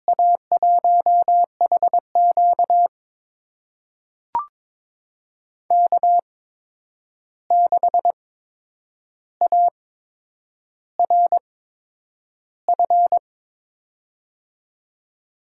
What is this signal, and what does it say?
Morse code, heard as a keyed tone, about 22 wpm, spaced as for 9 wpm, 700 Hz. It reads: A1HQ K6ARF
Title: USA Call Signs - 5 Characters - 8x Character Spacing - 22wpm